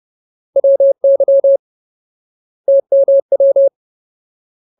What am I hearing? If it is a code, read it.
Morse code, heard as a keyed tone, about 30 wpm, 550 Hz. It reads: WY TMW
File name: Ultimate Word List / 30WPM Ultimate